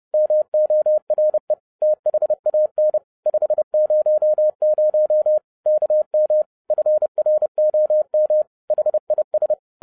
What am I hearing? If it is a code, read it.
Morse code, heard as a keyed tone, about 30 wpm, 610 Hz. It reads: MORE THAN 500 KM FROM HIS